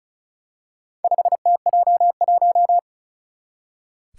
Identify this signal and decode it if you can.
Morse code, heard as a keyed tone, about 36 wpm, 690 Hz. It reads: 5TJ1
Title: Four Challenging Letter-Number Combinations 35wpm